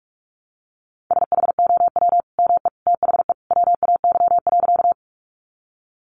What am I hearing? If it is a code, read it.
Morse code, heard as a keyed tone, about 44 wpm, 700 Hz. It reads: SHOW ME THE WAY.
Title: Sentences from Top 200 Words 45wpm